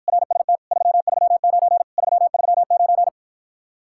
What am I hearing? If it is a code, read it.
Morse code, heard as a keyed tone, about 52 wpm, 700 Hz. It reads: RST 339 339